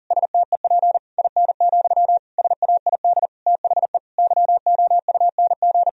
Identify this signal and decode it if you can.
Morse code, heard as a keyed tone, about 40 wpm, 700 Hz. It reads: STEP IN, SAID THE YOUNG
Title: Sentences from Top 1000 Words - Review Part 2 - 40wpm